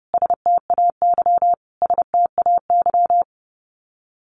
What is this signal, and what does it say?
Morse code, heard as a keyed tone, about 30 wpm, 700 Hz. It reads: STAY STAY
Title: Top 500 Words Repeated 30wpm